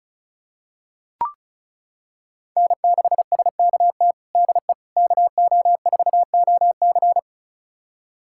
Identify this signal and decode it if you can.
Morse code, heard as a keyed tone, about 36 wpm, 700 Hz. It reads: N6SKT DE KO4OC
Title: USA Callsign DE Callsign - 5 Characters - 35wpm